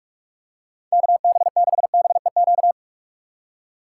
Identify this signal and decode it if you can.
Morse code, heard as a keyed tone, about 43 wpm, 700 Hz. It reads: KB6BEQ
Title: USA Call Signs - 2x Character Spacing - 45wpm